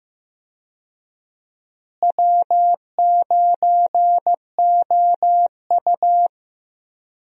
Morse code, heard as a keyed tone, about 15 wpm, 700 Hz. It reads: W9OU
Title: USA Call Signs - 4 Characters - Repeated 2x - 15wpm